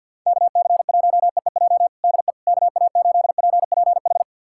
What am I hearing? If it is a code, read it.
Morse code, heard as a keyed tone, about 52 wpm, 700 Hz. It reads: KC1EEJ DE KA8PPH
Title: USA Callsign DE Callsign - 6 Characters - 50wpm